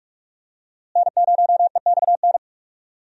Morse code, heard as a keyed tone, about 46 wpm, 700 Hz. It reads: N0EXN
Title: USA Call Signs - 5 Characters - Repeated 2x - 45wpm